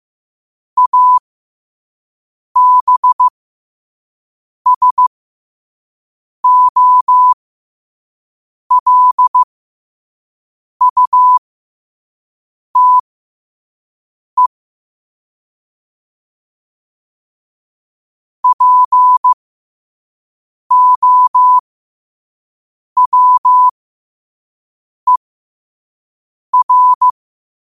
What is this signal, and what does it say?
Morse code, heard as a keyed tone, about 15 wpm, spaced as for 5 wpm, 1000 Hz. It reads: ABSOLUTE POWER
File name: qotdfiles / 5wpm